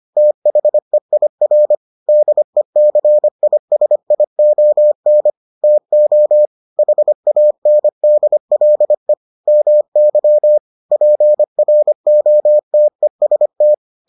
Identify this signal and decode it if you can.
Morse code, heard as a keyed tone, about 25 wpm, 600 Hz. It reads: THEIR DECISION TO HANDLE MY PROTEST